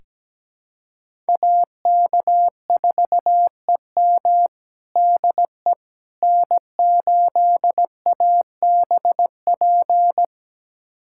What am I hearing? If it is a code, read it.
Morse code, heard as a keyed tone, about 17 wpm, 700 Hz. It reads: AK4EM DE N8ABP
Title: USA Callsign DE Callsign - 5 Characters - 17wpm